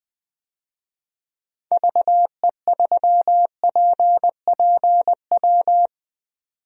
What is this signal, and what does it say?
Morse code, heard as a keyed tone, about 20 wpm, 700 Hz. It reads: VE3PPW
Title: DXCC-100 Call Signs - 6 Characters - 20wpm